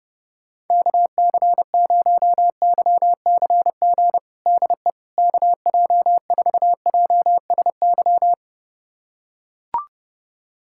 Morse code, heard as a keyed tone, about 30 wpm, 700 Hz. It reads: KC0YCG DE KJ4JSY
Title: USA Callsign DE Callsign - 6 Characters - 30wpm